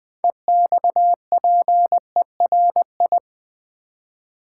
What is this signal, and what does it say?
Morse code, heard as a keyed tone, about 20 wpm, 700 Hz. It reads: EXPERI
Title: Build A Word - 6 Letters At A Time - 12-Letter Words - 20wpm